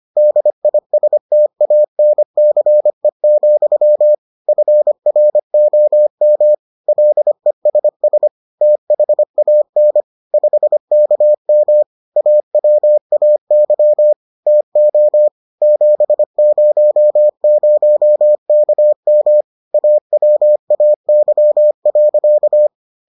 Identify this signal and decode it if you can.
Morse code, heard as a keyed tone, about 25 wpm, 600 Hz. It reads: DISTANCE, FROM LESS THAN 5KM AWAY TO 700KM AWAY.